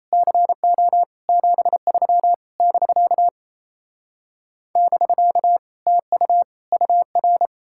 Morse code, heard as a keyed tone, about 37 wpm, 700 Hz. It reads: CO 73 <BK> <BK> TU UR